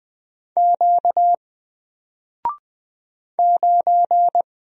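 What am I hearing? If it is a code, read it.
Morse code, heard as a keyed tone, about 20 wpm, 700 Hz. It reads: Q 9
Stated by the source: Single letter and number Morse drill at 20 wpm, from N4LCD.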